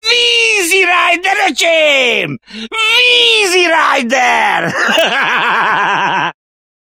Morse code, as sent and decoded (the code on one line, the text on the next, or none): none